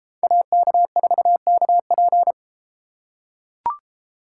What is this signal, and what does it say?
Morse code, heard as a keyed tone, about 33 wpm, 700 Hz. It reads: AK4KP